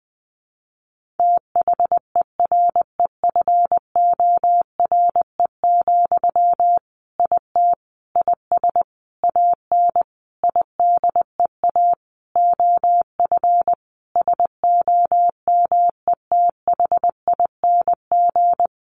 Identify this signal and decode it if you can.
Morse code, heard as a keyed tone, about 20 wpm, 700 Hz. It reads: THEREFORE, IT IS AN IDEA OF SOMETHING
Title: Sentences from Top 1500 Words 20wpm